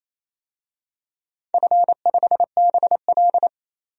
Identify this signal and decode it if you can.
Morse code, heard as a keyed tone, about 27 wpm, 700 Hz. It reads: F5BL
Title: DXCC-100 Call Signs - 4 Characters - 28wpm